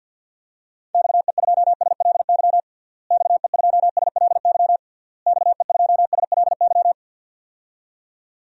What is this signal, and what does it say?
Morse code, heard as a keyed tone, about 51 wpm, 700 Hz. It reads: XE2SLY XE2SLY XE2SLY
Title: DXCC-100 Call Signs - 6 Characters - Repeated 3x - 50wpm